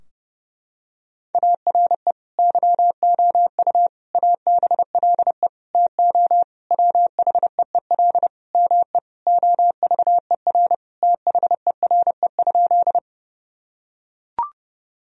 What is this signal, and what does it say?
Morse code, heard as a keyed tone, about 32 wpm, 700 Hz. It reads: ARE YOU ABLE TO WHEEL ME OVER THERE?
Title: Sentences from Top 1000 Words - Review Part 3 - 30wpm